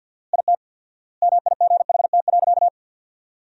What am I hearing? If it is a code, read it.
Morse code, heard as a keyed tone, about 52 wpm, 700 Hz. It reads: IT MIGHT.
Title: 4x Word-Spacing - Sentences from Top 500 Words - 50wpm